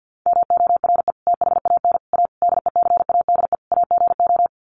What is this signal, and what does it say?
Morse code, heard as a keyed tone, about 50 wpm, 700 Hz. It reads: MORE THAN A DECADE AGO
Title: Sets of 5 Words Encore2 50wpm